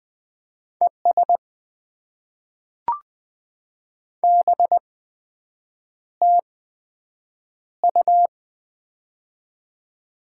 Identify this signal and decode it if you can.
Morse code, heard as a keyed tone, about 20 wpm, spaced as for 7 wpm, 700 Hz. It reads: ES BTU
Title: Top 20 QSO Elements - 8x Character Spacing - 20wpm